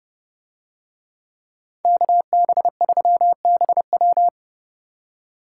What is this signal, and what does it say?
Morse code, heard as a keyed tone, about 28 wpm, 700 Hz. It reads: KB3BW